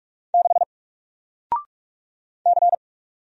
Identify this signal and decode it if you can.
Morse code, heard as a keyed tone, about 46 wpm, 700 Hz. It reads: 6 C
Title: Single Letter-Number 45wpm